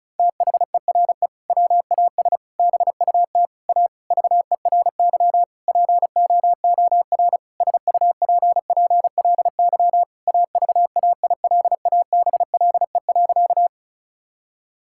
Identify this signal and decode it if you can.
Morse code, heard as a keyed tone, about 34 wpm, 700 Hz. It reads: THERE WAS BUT A VERY POOR SUPPLY AVAILABLE.
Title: Sentences from Top 1000 Words - Review Part 5 - 35wpm